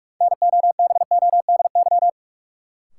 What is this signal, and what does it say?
Morse code, heard as a keyed tone, about 44 wpm, 700 Hz. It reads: NOBODY
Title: Sentences from Top 1600 Words - New Word Drill Alt - 45wpm